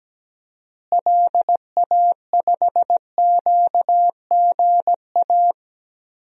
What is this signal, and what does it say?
Morse code, heard as a keyed tone, about 17 wpm, 700 Hz. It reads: LA5QGA
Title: DXCC-100 Call Signs - 6 Characters - Repeated 2x - 17wpm